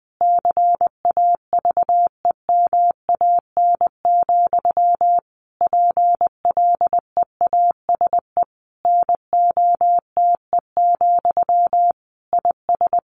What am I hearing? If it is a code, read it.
Morse code, heard as a keyed tone, about 20 wpm, 700 Hz. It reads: CAVEMAN, PLEASE NOTE, IS
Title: A Sound Of Thunder 20wpm